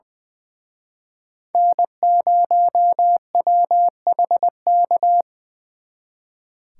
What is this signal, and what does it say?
Morse code, heard as a keyed tone, about 24 wpm, 700 Hz. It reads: N0WHK
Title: Speed Racing - USA 1x3 Call Signs 20wpm